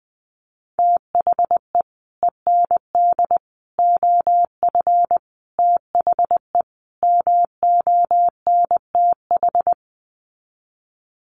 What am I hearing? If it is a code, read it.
Morse code, heard as a keyed tone, about 20 wpm, 700 Hz. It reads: THE END OF THE MONTH
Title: Sets of 5 Words Encore 20wpm